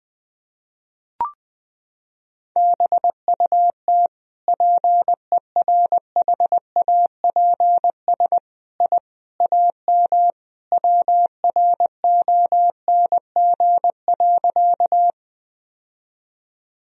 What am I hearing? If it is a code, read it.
Morse code, heard as a keyed tone, about 20 wpm, 700 Hz. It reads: BUT PERHAPS I AM WRONG.
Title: Sentences from Top 1000 Words - Review Part 2 - 20wpm